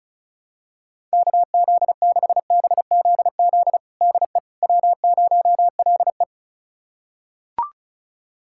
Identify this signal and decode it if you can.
Morse code, heard as a keyed tone, about 36 wpm, 700 Hz. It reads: KZ6BZZ DE W0LE
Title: USA Callsign DE Callsign 35wpm